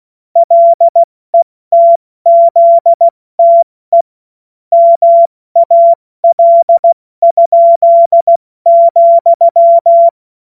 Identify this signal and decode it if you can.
Morse code, heard as a keyed tone, about 16 wpm, 680 Hz. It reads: LETZTE MAL?,
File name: Morsekurs / Witze / Witz 10 Juni Mai 2025 70 F80